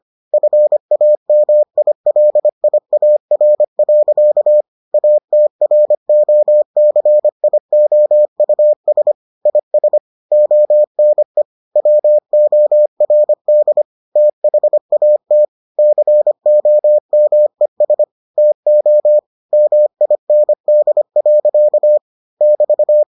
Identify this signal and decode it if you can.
Morse code, heard as a keyed tone, about 25 wpm, 600 Hz. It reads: FAMILIAR. ATROCIOUS IS ONE WORD THAT COMES TO MIND. <BT>